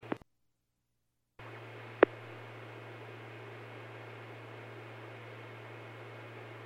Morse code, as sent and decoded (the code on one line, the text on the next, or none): none